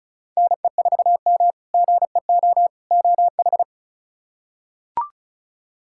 Morse code, heard as a keyed tone, about 36 wpm, 700 Hz. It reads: NE4M GEO OH
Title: K1USN SST QSO Practice - 35wpm